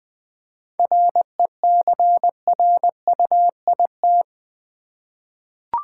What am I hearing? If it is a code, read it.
Morse code, heard as a keyed tone, about 20 wpm, 700 Hz. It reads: RECRUIT
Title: 7 Letter Words - First 3 Letters Given - 20wpm